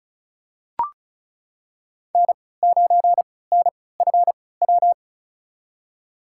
Morse code, heard as a keyed tone, about 36 wpm, spaced as for 20 wpm, 700 Hz. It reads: N9NFW
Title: USA Call Signs - 3x Character Spacing - 35wpm